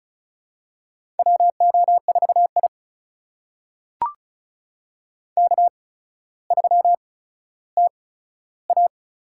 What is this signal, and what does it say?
Morse code, heard as a keyed tone, about 36 wpm, spaced as for 15 wpm, 700 Hz. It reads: WO4I K3TA
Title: USA Call Signs - 4 Characters - 8x Character Spacing - 35wpm